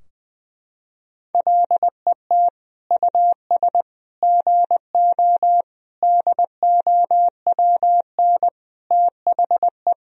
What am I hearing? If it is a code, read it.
Morse code, heard as a keyed tone, about 20 wpm, 700 Hz. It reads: LET US GO DOWN THE
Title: Sentences from Top 1000 Words - Review Part 3 - 20wpm